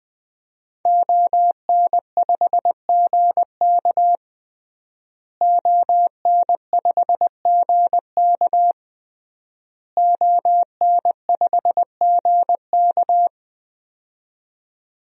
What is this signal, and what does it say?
Morse code, heard as a keyed tone, about 20 wpm, 700 Hz. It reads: ON5GK ON5GK ON5GK
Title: DXCC-100 Call Signs - 5 Characters - Repeated 3x - 20wpm